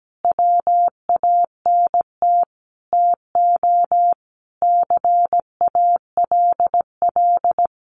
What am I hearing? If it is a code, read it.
Morse code, heard as a keyed tone, about 17 wpm, 700 Hz. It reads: WANT TO CALL
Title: Sets of 5 Words Encore2 17wpm